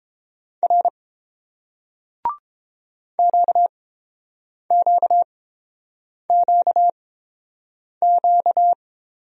Morse code, heard as a keyed tone, about 31 wpm, spaced as for 7 wpm, 700 Hz. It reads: R QQQQ